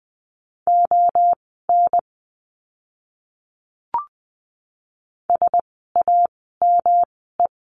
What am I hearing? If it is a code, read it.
Morse code, heard as a keyed tone, about 20 wpm, spaced as for 14 wpm, 700 Hz. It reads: ON SAME
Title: 192 Lesson 31f - Letter v - Words 20wpm